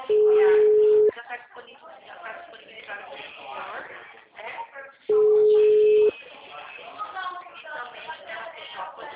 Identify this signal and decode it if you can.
Ringback tone (Europe, 425 Hz), with speech.